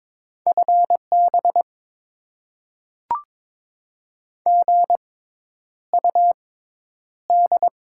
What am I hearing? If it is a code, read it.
Morse code, heard as a keyed tone, about 22 wpm, spaced as for 10 wpm, 700 Hz. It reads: FB GUD
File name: Top 20 QSO Elements - 6x Character Spacing - 22wpm